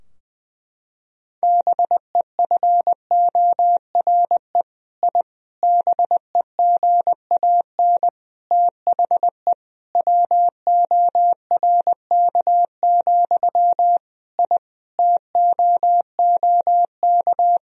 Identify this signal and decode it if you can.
Morse code, heard as a keyed tone, about 20 wpm, 700 Hz. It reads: BEFORE I BEGAN THE WORK, I TOOK
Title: Sentences from Top 1000 Words - Review Part 4 - 20wpm